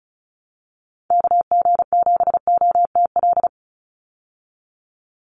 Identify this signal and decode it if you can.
Morse code, heard as a keyed tone, about 34 wpm, 700 Hz. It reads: KG7OTL
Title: USA Call Signs 35wpm